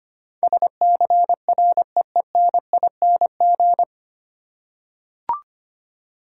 Morse code, heard as a keyed tone, about 25 wpm, 700 Hz. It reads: SCREENING
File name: Nine Letter Words - 25wpm